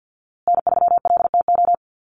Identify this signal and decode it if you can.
Morse code, heard as a keyed tone, about 50 wpm, 700 Hz. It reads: N3LTO
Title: USA Call Signs Repeated 3x 50wpm